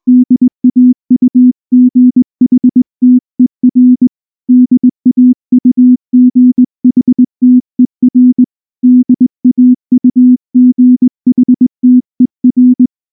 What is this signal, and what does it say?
Morse code, heard as a keyed tone, about 19 wpm, 250 Hz. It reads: DAUGHTER DAUGHTER DAUGHTER
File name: morse / morse-training-20WPM-1